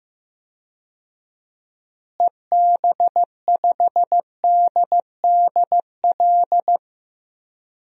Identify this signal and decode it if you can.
Morse code, heard as a keyed tone, about 15 wpm, 700 Hz. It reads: EB5DDL
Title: DXCC-100 Call Signs - 6 Characters - Repeated 2x - 15wpm